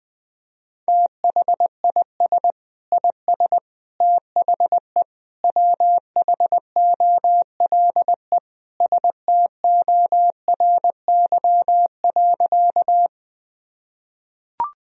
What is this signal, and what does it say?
Morse code, heard as a keyed tone, about 21 wpm, 700 Hz. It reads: THIS IS THE WHOLE STORY.
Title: Sentences from Top 1000 Words - Review Part 1 - 20wpm